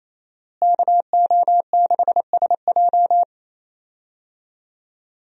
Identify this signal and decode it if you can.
Morse code, heard as a keyed tone, about 28 wpm, 700 Hz. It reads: KO6SJ
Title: USA Call Signs - 5 Characters - Repeated 2x - 28wpm